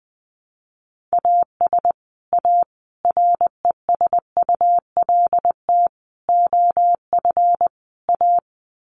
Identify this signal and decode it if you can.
Morse code, heard as a keyed tone, about 20 wpm, 700 Hz. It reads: AS A RESULT OF A
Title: Sets of 5 Words Encore 20wpm